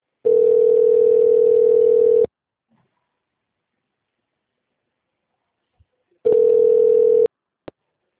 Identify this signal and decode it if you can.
Ringback tone (North America), 440 + 480 Hz.
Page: Itapetininga